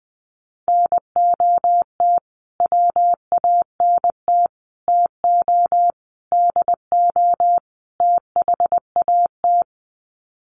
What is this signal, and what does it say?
Morse code, heard as a keyed tone, about 19 wpm, 700 Hz. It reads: NOT WANT TO DO THAT
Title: Sets of 5 Words Encore 20wpm